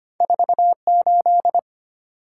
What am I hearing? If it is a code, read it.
Morse code, heard as a keyed tone, about 25 wpm, 700 Hz. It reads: 48
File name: Two Letter-Number Combinations - 25wpm